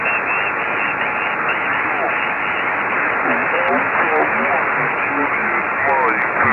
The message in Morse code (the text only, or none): none